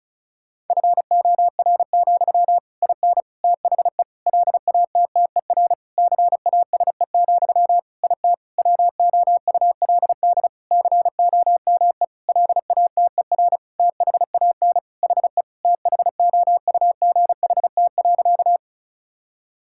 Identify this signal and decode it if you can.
Morse code, heard as a keyed tone, about 36 wpm, 700 Hz. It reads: FOR, IN THE LATTER CASE, IT WOULD COME LATER THAN HE THOUGHT.
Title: Sentences from Top 1600 Words 35wpm